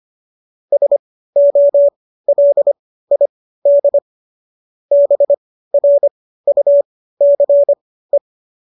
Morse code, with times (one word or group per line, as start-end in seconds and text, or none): 0.7-4.0 SOLID
4.9-8.2 BRUCE